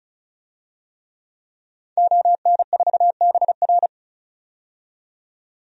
Morse code, heard as a keyed tone, about 34 wpm, 700 Hz. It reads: ON4BR
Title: DXCC-100 Call Signs - 5 Characters - 35wpm